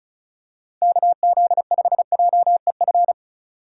Morse code, heard as a keyed tone, about 36 wpm, 700 Hz. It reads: KZ5JEF
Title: USA Call Signs - 6 Characters - 4x Character Spacing - 35wpm